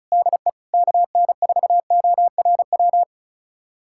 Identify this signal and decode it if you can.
Morse code, heard as a keyed tone, about 36 wpm, 700 Hz. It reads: DE KN4ORW
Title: USA Callsign DE Callsign - 6 Characters - 35wpm